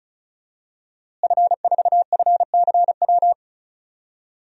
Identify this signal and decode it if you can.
Morse code, heard as a keyed tone, about 36 wpm, 700 Hz. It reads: F4FCW
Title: DXCC-100 Call Signs - 5 Characters - Repeated 3x - 35wpm